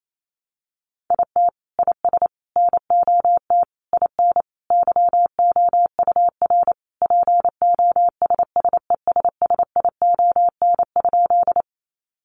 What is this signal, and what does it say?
Morse code, heard as a keyed tone, about 28 wpm, 700 Hz. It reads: IT IS NOT IN YOUR POSSESSION?
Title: Sentences from Top 1400 Words 28wpm